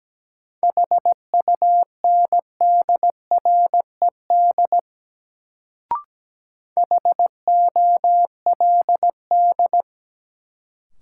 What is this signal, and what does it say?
Morse code, heard as a keyed tone, about 17 wpm, 700 Hz. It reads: HUNDRED HOLD
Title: Sentences from Top 400 Words - New Word Drill - 17wpm